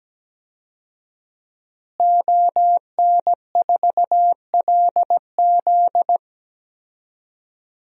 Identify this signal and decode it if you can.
Morse code, heard as a keyed tone, about 17 wpm, 700 Hz. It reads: ON4LZ
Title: DXCC-100 Call Signs - 5 Characters - Repeated 3x - 17wpm